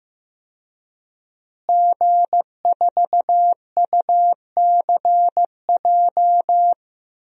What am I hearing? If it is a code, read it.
Morse code, heard as a keyed tone, about 15 wpm, 700 Hz. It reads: G4UCJ
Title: DXCC-100 Call Signs - 5 Characters - Repeated 2x - 15wpm